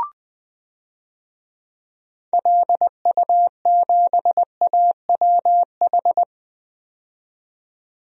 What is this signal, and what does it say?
Morse code, heard as a keyed tone, about 21 wpm, 700 Hz. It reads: LU7AWH
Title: DXCC-100 Call Signs - 6 Characters - 20wpm